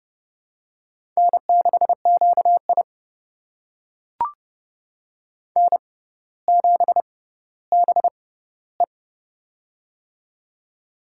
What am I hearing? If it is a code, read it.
Morse code, heard as a keyed tone, about 32 wpm, spaced as for 16 wpm, 700 Hz. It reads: N6QI N7BE
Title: USA Call Signs - 6x Character Spacing - 30wpm